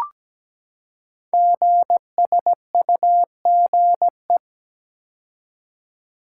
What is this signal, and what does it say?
Morse code, heard as a keyed tone, about 17 wpm, 700 Hz. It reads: GSUGE
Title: Code Groups - 5 Letters - 17wpm